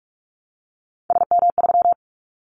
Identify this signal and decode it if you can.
Morse code, heard as a keyed tone, about 44 wpm, 700 Hz. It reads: SM3
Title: Callsign Prefixes for Top 25 DXCC - Repeated 3x - 45wpm